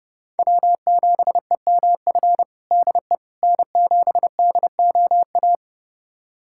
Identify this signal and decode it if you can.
Morse code, heard as a keyed tone, about 30 wpm, 700 Hz. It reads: W7EMF DE N7DOA
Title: USA Callsign DE Callsign - 5 Characters - 30wpm